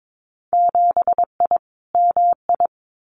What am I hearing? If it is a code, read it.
Morse code, heard as a keyed tone, about 22 wpm, 700 Hz. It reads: 7I MI
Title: Winter Field Day Exchanges Encore 22wpm